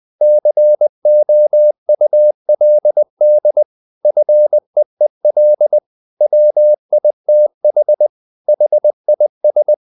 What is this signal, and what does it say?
Morse code, heard as a keyed tone, about 20 wpm, 600 Hz. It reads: COULD FEEL WITH HIS
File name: the old man and the sea part 2 20wpm